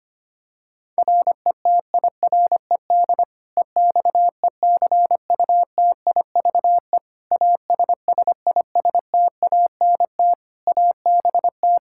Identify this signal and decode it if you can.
Morse code, heard as a keyed tone, about 25 wpm, 700 Hz. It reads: RETIRED EXECUTIVE ASSISTANT ABT
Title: Practice QSO - Exchange 3 - More Info - 25wpm